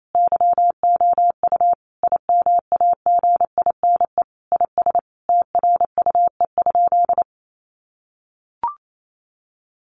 Morse code, heard as a keyed tone, about 28 wpm, 700 Hz. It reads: YOU IMAGINE IS TRUE?
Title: Sentences from Top 1400 Words 28wpm